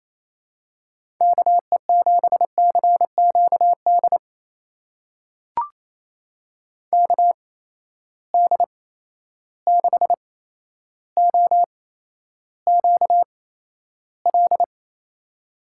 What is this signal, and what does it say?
Morse code, heard as a keyed tone, about 28 wpm, spaced as for 12 wpm, 700 Hz. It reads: KE7CQD KD6OQL
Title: USA Call Signs - 6 Characters - 8x Character Spacing - 28wpm